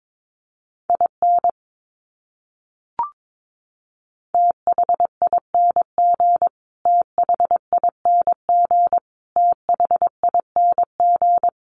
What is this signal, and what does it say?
Morse code, heard as a keyed tone, about 22 wpm, 700 Hz. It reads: IN THING THING THING
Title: Top 200 Words Repeated 22wpm